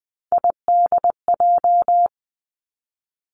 Morse code, heard as a keyed tone, about 20 wpm, 700 Hz. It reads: IDJ
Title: Three Challenging Letter-Number Combinations 20wpm